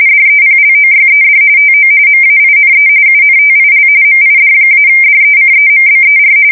RTTY, about 45.45 baud. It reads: DEPLOYED TO MAINE UNTIL MID-DECEMBER
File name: fsk170 / USS-New-Jersey